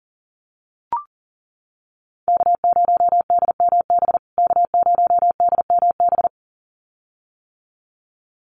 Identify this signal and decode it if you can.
Morse code, heard as a keyed tone, about 40 wpm, 700 Hz. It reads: K0DMB K0DMB
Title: USA Call Signs Repeated 2x 40wpm